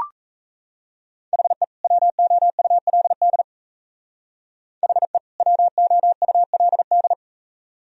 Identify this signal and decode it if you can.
Morse code, heard as a keyed tone, about 37 wpm, 690 Hz. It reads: HE WOULD HE WOULD